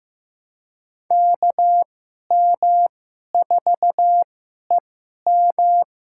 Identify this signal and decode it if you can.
Morse code, heard as a keyed tone, about 15 wpm, spaced as for 11 wpm, 700 Hz. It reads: KM4EM
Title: USA Call Signs - 5 Characters - 2x Character Spacing - 15wpm